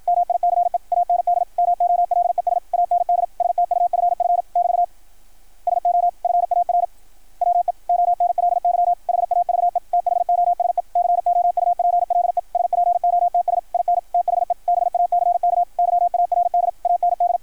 Morse code, heard as a keyed tone, about 53 wpm, 690 Hz. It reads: MICE AND MOLES AND STUFF <BT> SO FAR WE ONLY HAVE THOSE COUPLE SPOTS IN THE BACK YARD AND
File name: archived / CRSnet